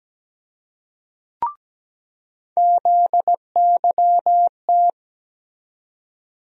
Morse code, heard as a keyed tone, about 17 wpm, 700 Hz. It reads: ZYT
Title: USA Call Sign Postfixes - 3 Letters 17wpm